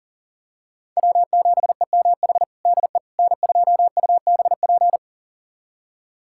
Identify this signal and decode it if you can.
Morse code, heard as a keyed tone, about 37 wpm, 700 Hz. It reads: W7EMH DE N2UBP
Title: USA Callsign DE Callsign - 5 Characters - 40wpm